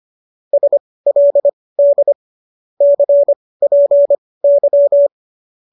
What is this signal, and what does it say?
Morse code, heard as a keyed tone, about 25 wpm, spaced as for 18 wpm, 590 Hz. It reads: SLD CPY
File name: qso207 18